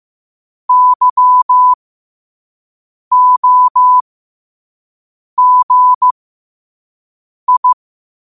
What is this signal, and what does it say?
Morse code, heard as a keyed tone, about 15 wpm, spaced as for 5 wpm, 1000 Hz. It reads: YOGI
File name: qotdfiles / 5wpm